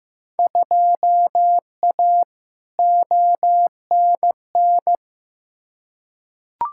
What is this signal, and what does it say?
Morse code, heard as a keyed tone, about 15 wpm, 700 Hz. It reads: 2A ONN
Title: Field Day Exchanges Encore - 15wpm